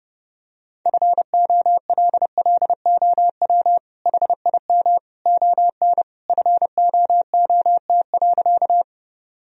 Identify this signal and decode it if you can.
Morse code, heard as a keyed tone, about 30 wpm, 700 Hz. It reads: FOLLOW HIM ON FOOT.
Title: Sentences from Top 1000 Words - Review Part 3 - 30wpm